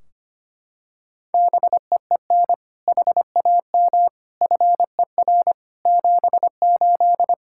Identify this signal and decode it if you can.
Morse code, heard as a keyed tone, about 25 wpm, 700 Hz. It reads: BEEN HAM FER 78
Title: K1USN SST QSO Practice 25wpm